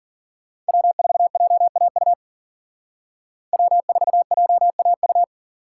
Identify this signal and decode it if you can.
Morse code, heard as a keyed tone, about 48 wpm, 700 Hz. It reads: W4JAU W4JAU
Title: Speed Racing - USA Call Signs 40wpm